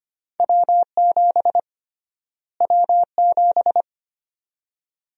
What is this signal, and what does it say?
Morse code, heard as a keyed tone, about 25 wpm, 700 Hz. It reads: W7 W7
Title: USA Call Sign Prefixes - Repeated 2x - 25wpm